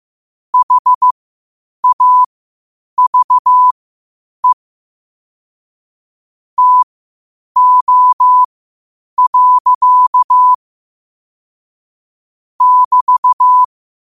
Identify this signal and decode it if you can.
Morse code, heard as a keyed tone, about 15 wpm, spaced as for 8 wpm, 1000 Hz. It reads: HAVE TO. <BT>
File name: qotdfiles / 8wpm